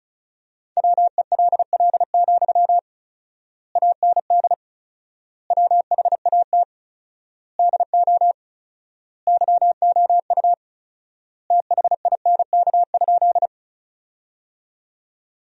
Morse code, heard as a keyed tone, about 36 wpm, 700 Hz. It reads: WELL, AND WHAT DO YOU THINK?
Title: 4x Word-Spacing - Sentences from Top 200 Words - 35wpm